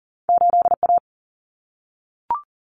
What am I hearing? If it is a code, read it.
Morse code, heard as a keyed tone, about 40 wpm, 700 Hz. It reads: 8A
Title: Common Cut Numbers - 2-Digit 40wpm